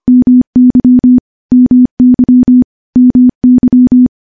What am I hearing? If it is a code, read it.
Morse code, heard as a keyed tone, about 25 wpm, 250 Hz. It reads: MY MY MY
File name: morse-training-25WPM